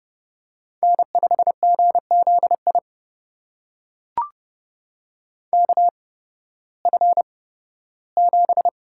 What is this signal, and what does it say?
Morse code, heard as a keyed tone, about 31 wpm, spaced as for 16 wpm, 700 Hz. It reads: N5GZI KF7
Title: USA Call Signs - 5 Characters - 8x Character Spacing - 30wpm